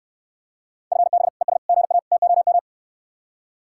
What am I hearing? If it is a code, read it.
Morse code, heard as a keyed tone, about 73 wpm, 700 Hz. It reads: FB ES GA TOM